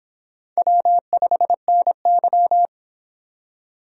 Morse code, heard as a keyed tone, about 26 wpm, 700 Hz. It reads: W5NY